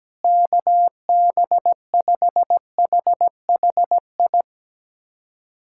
Morse code, heard as a keyed tone, about 17 wpm, 700 Hz. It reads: KB5HHI
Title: USA Callsign DE Callsign - 6 Characters - 17wpm